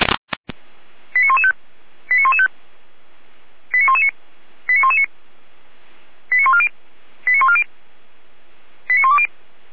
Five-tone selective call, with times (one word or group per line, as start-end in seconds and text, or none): 1.1-1.5 89115
2.1-2.5 89115
3.7-4.1 89119
4.7-5.1 89119
6.3-6.7 89140
7.3-7.6 89140
8.9-9.3 89120